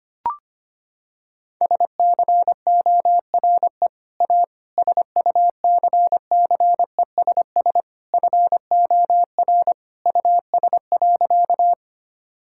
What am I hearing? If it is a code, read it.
Morse code, heard as a keyed tone, about 25 wpm, 700 Hz. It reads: SCORE A SUCCESS FOR US.
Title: Sentences from Top 1600 Words 25wpm